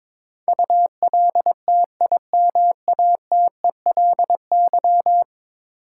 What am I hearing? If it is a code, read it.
Morse code, heard as a keyed tone, about 22 wpm, 700 Hz. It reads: ULTIMATELY
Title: Sentences from Top 1200 Words - New Word Drill Alt - 22wpm